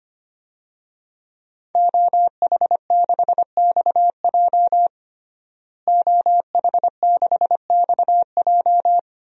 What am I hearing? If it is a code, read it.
Morse code, heard as a keyed tone, about 25 wpm, 700 Hz. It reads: OH6XJ OH6XJ
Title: DXCC-100 Call Signs - 5 Characters - Repeated 2x - 25wpm